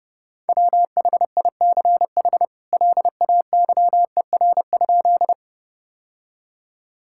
Morse code, heard as a keyed tone, about 30 wpm, 700 Hz. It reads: WHICH LAYER?
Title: Sentences from Top 1600 Words 30wpm